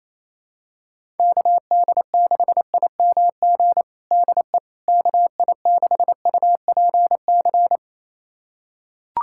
Morse code, heard as a keyed tone, about 28 wpm, 700 Hz. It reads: KD6IMG DE KI6UPC
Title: USA Callsign DE Callsign 28wpm